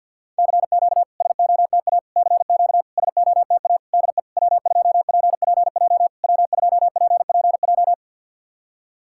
Morse code, heard as a keyed tone, about 50 wpm, 700 Hz. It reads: CQ SOTA CQ SOTA DE W2PPJ W2PPJ
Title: SOTA QSO Practice 50wpm